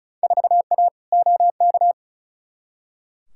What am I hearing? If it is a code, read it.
Morse code, heard as a keyed tone, about 36 wpm, 700 Hz. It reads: 4A OK
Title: Field Day Exchanges - 35wpm